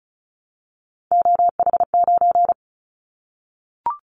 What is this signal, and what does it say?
Morse code, heard as a keyed tone, about 36 wpm, 700 Hz. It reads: OH9
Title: Callsign Prefixes for Top 25 DXCC - 35wpm